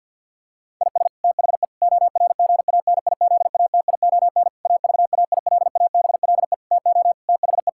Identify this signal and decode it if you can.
Morse code, heard as a keyed tone, about 50 wpm, 700 Hz. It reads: IS THE ORGANIZATION AVAILABLE TO THE
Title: Sentences from Top 1000 Words - Review Part 5 - 50wpm